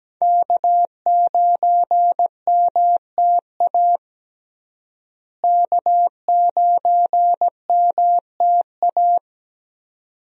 Morse code, heard as a keyed tone, about 17 wpm, 700 Hz. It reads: K9MTA K9MTA
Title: USA Call Signs - 5 Characters - Repeated 2x - 17wpm